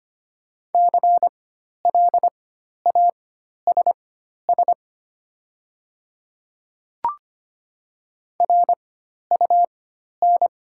Morse code, heard as a keyed tone, about 25 wpm, spaced as for 12 wpm, 700 Hz. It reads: CLASS RUN
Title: 4x Character Spacing - Top 500 Words - 25wpm